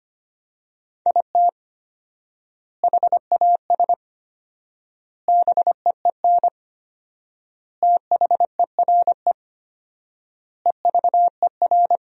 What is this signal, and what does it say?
Morse code, heard as a keyed tone, about 25 wpm, 700 Hz. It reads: IT HAS BEEN THERE EVER
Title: 4x Word-Spacing - Sentences from Top 400 Words - 25wpm